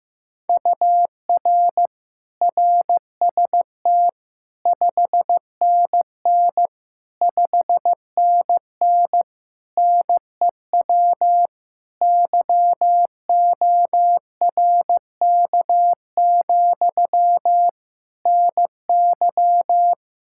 Morse code, heard as a keyed tone, about 15 wpm, 700 Hz. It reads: UR RST 5NN 5NN NEW YORK, NY
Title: Practice QSO - Exchange 1 - Beginning - 15wpm